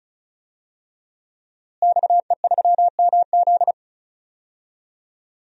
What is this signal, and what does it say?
Morse code, heard as a keyed tone, about 34 wpm, 700 Hz. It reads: XE3MZ
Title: DXCC-100 Call Signs - 5 Characters - Repeated 2x - 35wpm